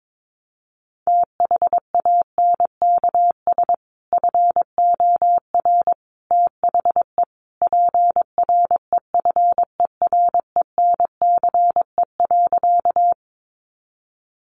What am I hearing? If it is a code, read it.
Morse code, heard as a keyed tone, about 22 wpm, 700 Hz. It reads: THANKS FOR THE PREFERENCE.
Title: Sentences from Top 1500 Words 22wpm